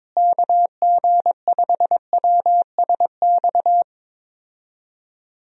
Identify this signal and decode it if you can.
Morse code, heard as a keyed tone, about 22 wpm, 700 Hz. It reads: KG5WSX
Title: USA Call Signs - 6 Characters - Repeated 3x - 22wpm